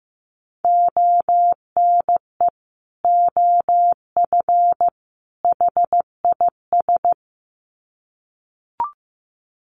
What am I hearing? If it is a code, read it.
Morse code, heard as a keyed tone, about 15 wpm, 700 Hz. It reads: ONE OF HIS
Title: Sets of 3 Words 15wpm